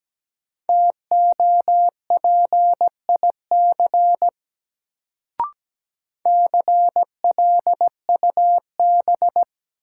Morse code, heard as a keyed tone, about 17 wpm, 700 Hz. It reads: TOPIC CLUB
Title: Sentences from Top 1200 Words - New Word Drill Alt - 17wpm